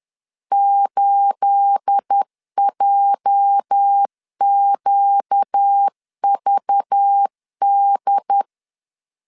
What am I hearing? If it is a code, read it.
Morse code, heard as a keyed tone, about 10 wpm, 800 Hz. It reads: 8JQVD
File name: CW / DET / 1 9 1500C S C 10wpm 4557